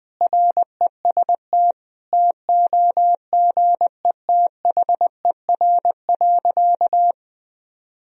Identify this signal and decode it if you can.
Morse code, heard as a keyed tone, about 21 wpm, 700 Hz. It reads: REST TOGETHER.
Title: Sentences from Top 500 Words and Abbreviations 20wpm